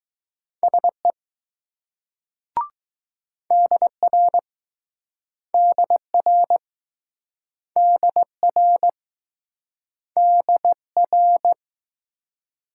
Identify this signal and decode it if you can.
Morse code, heard as a keyed tone, about 21 wpm, 700 Hz. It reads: SE DR DR DR DR